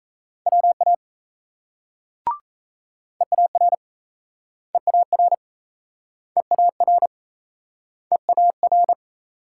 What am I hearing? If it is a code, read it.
Morse code, heard as a keyed tone, about 38 wpm, 700 Hz. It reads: WA EAR EAR EAR EAR